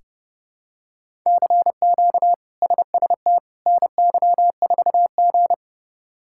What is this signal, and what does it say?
Morse code, heard as a keyed tone, about 30 wpm, 700 Hz. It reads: CQ SST NY4G
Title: K1USN SST QSO Practice - 30wpm